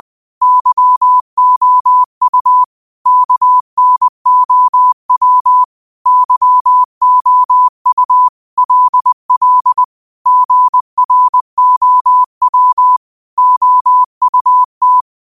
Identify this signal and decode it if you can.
Morse code, heard as a keyed tone, about 20 wpm, 1000 Hz. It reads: YOU KNOW YOU LL GROW OUT
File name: 20wpm